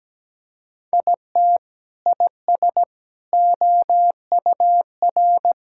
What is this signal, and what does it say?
Morse code, heard as a keyed tone, about 17 wpm, 700 Hz. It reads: IT IS OUR
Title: Sentences from Top 1600 Words 17wpm